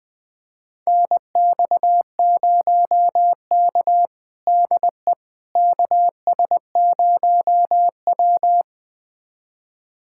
Morse code, heard as a keyed tone, about 21 wpm, 700 Hz. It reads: NX0K DE KS0W
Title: USA Callsign DE Callsign - 4 Characters - 20wpm